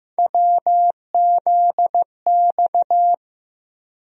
Morse code, heard as a keyed tone, about 15 wpm, 700 Hz. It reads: WZX
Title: USA Call Sign Postfixes - 3 Letters 15wpm